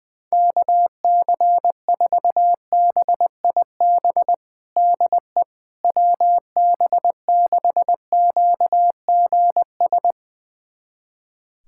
Morse code, heard as a keyed tone, about 20 wpm, 700 Hz. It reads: KC4BIB DE WB6QGS
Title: USA Callsign DE Callsign - 6 Characters - 20wpm